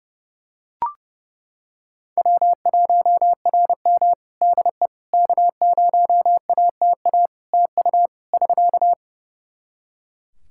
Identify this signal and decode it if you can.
Morse code, heard as a keyed tone, about 30 wpm, 700 Hz. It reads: W1RM DE K0ATA TU <SK>
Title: Practice QSO - Exchange 4 - Ending - 30wpm